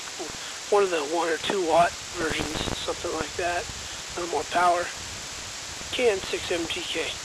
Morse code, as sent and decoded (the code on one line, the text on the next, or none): none